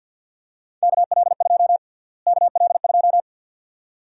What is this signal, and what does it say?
Morse code, heard as a keyed tone, about 48 wpm, 700 Hz. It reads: KL2 KL2
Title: USA Call Sign Prefixes - Repeated 2x - 50wpm